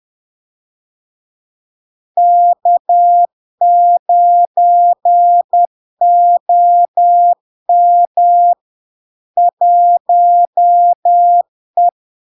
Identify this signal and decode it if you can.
Morse code, heard as a keyed tone, about 10 wpm, 700 Hz. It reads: K9OM 1E
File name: LARK-FD-2022-Log-in-CW-10wpm-700-Hz